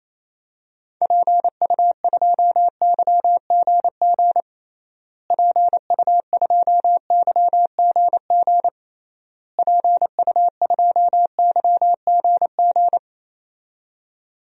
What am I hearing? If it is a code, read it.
Morse code, heard as a keyed tone, about 28 wpm, 700 Hz. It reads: PU2YGG PU2YGG PU2YGG
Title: DXCC-100 Call Signs - 6 Characters - Repeated 3x - 28wpm